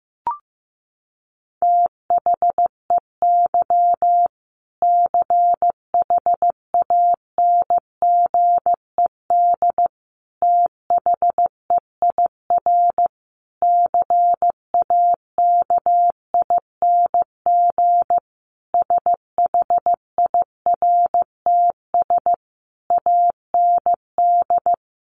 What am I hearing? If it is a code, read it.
Morse code, heard as a keyed tone, about 15 wpm, 700 Hz. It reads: THEY CHANGED THEIR CAKING SHIRTS AND
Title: A Sound Of Thunder 15wpm